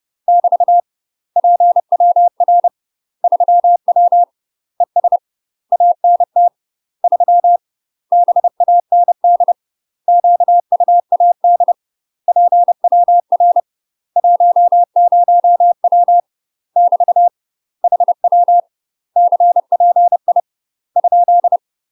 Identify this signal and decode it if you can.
Morse code, heard as a keyed tone, about 30 wpm, 700 Hz. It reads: <BT> PWR 3W ES ANT 3 BAND QUAD PWR 10W <BT> HW CPI ?
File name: UK MASTER QSOs TRAINING / 30WPM / UK MASTER QSOs 0041